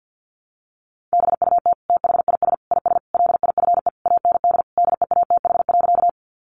Morse code, heard as a keyed tone, about 49 wpm, 700 Hz. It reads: BUT THIS IS LIFE AND DEATH.
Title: Sentences from Top 1100 Words 50wpm